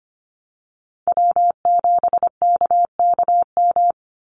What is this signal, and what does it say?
Morse code, heard as a keyed tone, about 25 wpm, 700 Hz. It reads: W7KKM
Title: NA SOTA Chaser Callsigns 25wpm